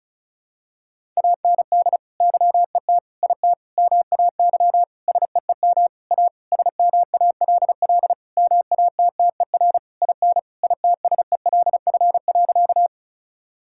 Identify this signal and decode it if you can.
Morse code, heard as a keyed tone, about 36 wpm, 700 Hz. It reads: AND YET IT MAY SEEM A SMALL MATTER IN ITSELF.
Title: Sentences from Top 1000 Words - Review Part 5 - 35wpm